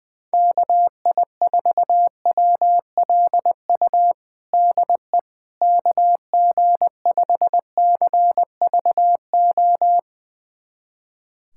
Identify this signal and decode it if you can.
Morse code, heard as a keyed tone, about 20 wpm, 700 Hz. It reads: KI4WLU DE KG5CVO
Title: USA Callsign DE Callsign - 6 Characters - 20wpm